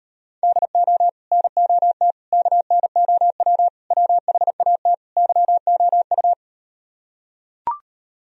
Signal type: Morse code

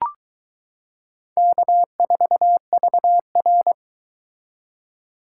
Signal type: Morse code

